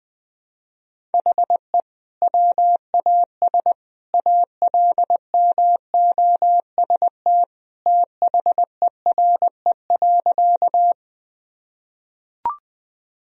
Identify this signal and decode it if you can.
Morse code, heard as a keyed tone, about 20 wpm, 700 Hz. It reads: HE WAS ALMOST THERE.
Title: Sentences from Top 1000 Words - Review Part 4 - 20wpm